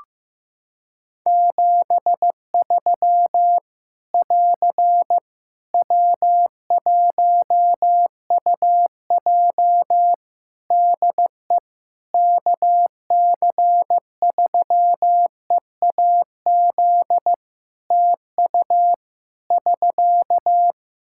Morse code, heard as a keyed tone, about 15 wpm, 700 Hz. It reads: 73 <AR> W1UJ DE KC3EAZ TU <SK>
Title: Practice QSO - Exchange 4 - Ending - 15wpm